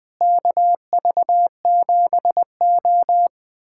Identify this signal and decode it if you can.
Morse code, heard as a keyed tone, about 20 wpm, 700 Hz. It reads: KV7O
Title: USA Callsign DE Callsign - 4 Characters - 20wpm